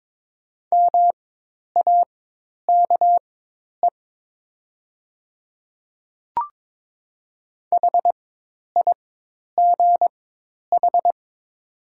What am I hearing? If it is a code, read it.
Morse code, heard as a keyed tone, about 22 wpm, spaced as for 10 wpm, 700 Hz. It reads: MAKE HIGH
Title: 4x Character Spacing - Top 200 Words - 22wpm